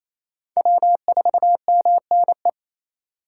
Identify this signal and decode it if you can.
Morse code, heard as a keyed tone, about 34 wpm, 700 Hz. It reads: W4MNE
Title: Speed Racing - USA 1x3 Call Signs 28wpm